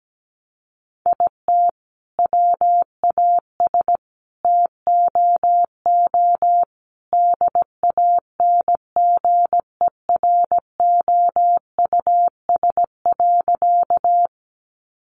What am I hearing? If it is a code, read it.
Morse code, heard as a keyed tone, about 17 wpm, 700 Hz. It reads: IT WAS TOO DANGEROUS.
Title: Sentences from Top 1200 Words 17wpm